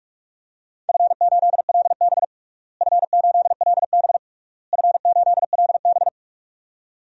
Morse code, heard as a keyed tone, about 46 wpm, 700 Hz. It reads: F8LB F8LB F8LB
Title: DXCC-100 Call Signs - 4 Characters - Repeated 3x - 45wpm